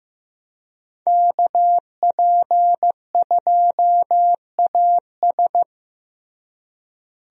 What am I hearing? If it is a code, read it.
Morse code, heard as a keyed tone, about 15 wpm, 700 Hz. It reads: KP2AS
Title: USA Call Signs - 5 Characters - Repeated 3x - 15wpm